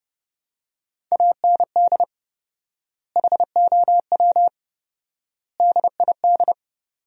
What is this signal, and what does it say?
Morse code, heard as a keyed tone, about 30 wpm, 700 Hz. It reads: AND HOW DID
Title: 4x Word-Spacing - Sentences from Top 100 Words - 30wpm